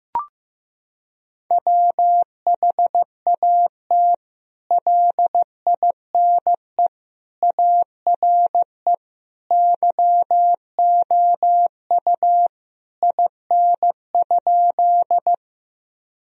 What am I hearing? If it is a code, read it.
Morse code, heard as a keyed tone, about 15 wpm, 700 Hz. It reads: WHAT LINE ARE YOU IN?
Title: Sentences from Top 1000 Words - Review Part 1 - 15wpm